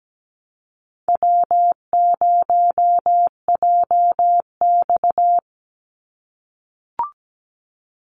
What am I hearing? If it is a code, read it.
Morse code, heard as a keyed tone, about 17 wpm, 700 Hz. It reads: W0JX
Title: Easy Call Signs - 17wpm